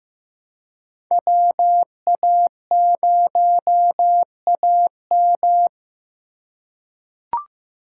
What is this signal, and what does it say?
Morse code, heard as a keyed tone, about 15 wpm, 700 Hz. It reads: WA0AM
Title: USA Call Signs - 5 Characters - 4x Character Spacing - 15wpm